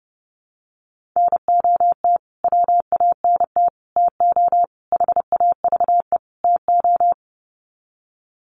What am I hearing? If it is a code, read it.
Morse code, heard as a keyed tone, about 30 wpm, 700 Hz. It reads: NOT WANT TO HAVE TO
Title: Sets of 5 Words Encore2 30wpm